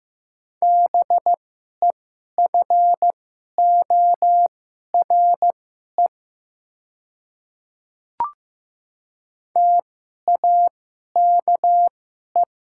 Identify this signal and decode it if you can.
Morse code, heard as a keyed tone, about 15 wpm, spaced as for 11 wpm, 700 Hz. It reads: BEFORE TAKE
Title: 2x Character Spacing - Top 200 Words - 15wpm